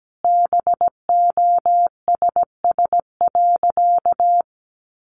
Morse code, heard as a keyed tone, about 17 wpm, 700 Hz. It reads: BOSS.
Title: Sentences from Top 700 Words 17wpm